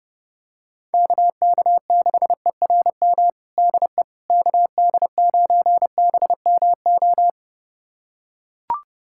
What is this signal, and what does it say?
Morse code, heard as a keyed tone, about 30 wpm, 700 Hz. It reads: KK6ERM DE KD9BMO
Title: USA Callsign DE Callsign - 6 Characters - 30wpm